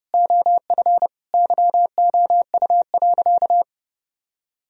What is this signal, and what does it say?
Morse code, heard as a keyed tone, about 30 wpm, 700 Hz. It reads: OF YOU.
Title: Sentences from Top 1000 Words - Review Part 1 - 30wpm